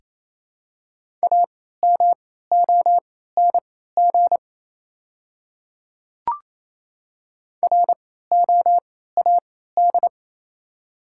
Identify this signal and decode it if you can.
Morse code, heard as a keyed tone, about 28 wpm, spaced as for 16 wpm, 700 Hz. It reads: AMONG ROAD
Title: 3x Character Spacing - Top 500 Words - 28wpm